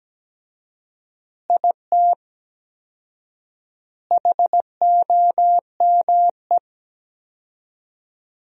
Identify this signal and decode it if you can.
Morse code, heard as a keyed tone, about 17 wpm, 700 Hz. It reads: IT HOME
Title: 4x Word-Spacing - Sentences from Top 500 Words - 17wpm